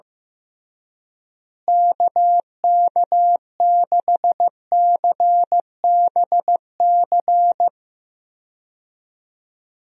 Morse code, heard as a keyed tone, about 15 wpm, 700 Hz. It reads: KK6CBC
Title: USA Call Signs - 6 Characters - Repeated 3x - 15wpm